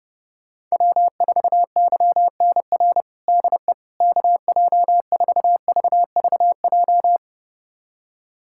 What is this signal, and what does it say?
Morse code, heard as a keyed tone, about 30 wpm, 700 Hz. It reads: W4YNR DE KJ4VVJ
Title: USA Callsign DE Callsign 30wpm